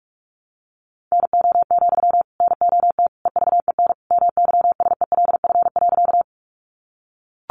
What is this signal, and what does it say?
Morse code, heard as a keyed tone, about 44 wpm, 700 Hz. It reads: NO, NOT EVEN MYSELF.
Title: Sentences from Top 1300 Words 45wpm